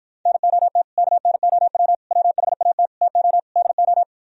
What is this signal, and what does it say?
Morse code, heard as a keyed tone, about 52 wpm, 700 Hz. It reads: NOT KNOW WHAT TO DO